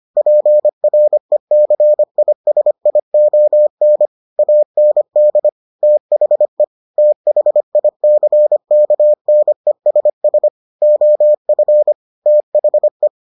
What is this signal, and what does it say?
Morse code, heard as a keyed tone, about 25 wpm, 600 Hz. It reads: PRECISION AND THE THICKNESS OF THE